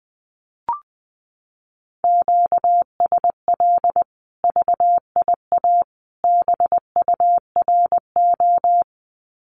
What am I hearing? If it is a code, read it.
Morse code, heard as a keyed tone, about 20 wpm, 700 Hz. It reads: QSL VIA BURO
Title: Common QSO Phrases 20wpm